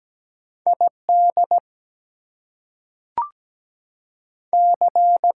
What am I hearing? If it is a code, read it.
Morse code, heard as a keyed tone, about 17 wpm, 700 Hz. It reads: ID C
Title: US States and Canadian Provinces 17wpm